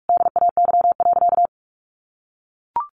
Morse code, heard as a keyed tone, about 44 wpm, 700 Hz. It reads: DAY.
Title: Sentences from Top 1100 Words 45wpm